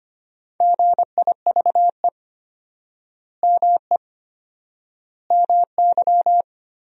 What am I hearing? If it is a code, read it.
Morse code, heard as a keyed tone, about 25 wpm, 700 Hz. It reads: GIVE ME MY
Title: 4x Word-Spacing - Sentences from Top 200 Words - 25wpm